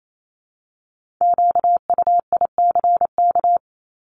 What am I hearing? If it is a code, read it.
Morse code, heard as a keyed tone, about 28 wpm, 700 Hz. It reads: QUICK
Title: Multi-letter Phonemes 28wpm